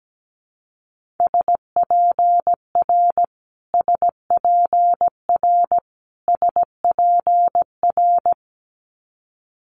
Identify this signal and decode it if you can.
Morse code, heard as a keyed tone, about 17 wpm, 700 Hz. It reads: SPR SPR SPR
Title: Multi-letter Phonemes Repeated 3x 17wpm